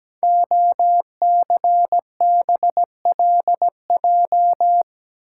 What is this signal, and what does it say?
Morse code, heard as a keyed tone, about 17 wpm, 700 Hz. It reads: OCBLJ
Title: Code Groups - 5 Letters - 17wpm